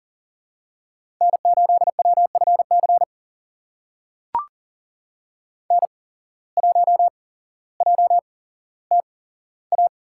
Morse code, heard as a keyed tone, about 39 wpm, spaced as for 17 wpm, 700 Hz. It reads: N8WFC N1JTA
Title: USA Call Signs - 5 Characters - 8x Character Spacing - 40wpm